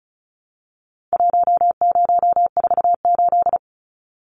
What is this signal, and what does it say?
Morse code, heard as a keyed tone, about 34 wpm, 700 Hz. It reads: 1048
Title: Numbers - Contest Style - 4-Digit 35wpm